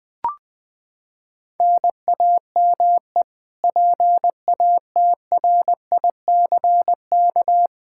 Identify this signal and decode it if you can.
Morse code, heard as a keyed tone, about 20 wpm, 700 Hz. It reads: NAME PATRICK
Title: K1USN SST QSO Practice 20wpm